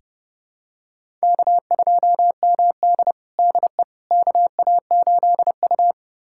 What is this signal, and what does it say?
Morse code, heard as a keyed tone, about 30 wpm, 700 Hz. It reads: K2MD DE KA8U
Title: USA Callsign DE Callsign - 4 Characters - 30wpm